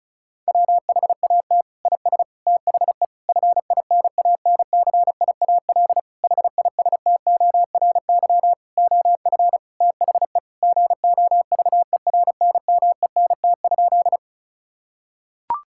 Morse code, heard as a keyed tone, about 36 wpm, 700 Hz. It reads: WHAT IS THE FINANCIAL HISTORY OF THE GOVERNMENT?
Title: Sentences from Top 1000 Words - Review Part 5 - 35wpm